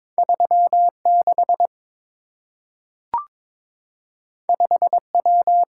Morse code, heard as a keyed tone, about 23 wpm, 700 Hz. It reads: 36 5W
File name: practice / 22wpm / Two Letter-Number Combinations - 22wpm